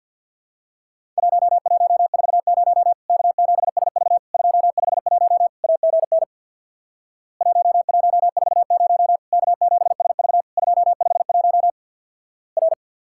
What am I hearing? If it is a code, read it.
Morse code, heard as a keyed tone, about 50 wpm, 700 Hz. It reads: 1140 K7SV 251 AGN 1140 K7SV 251 R
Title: Worked All Europe DX Contest - QTC Exchange with Repeats - 50wpm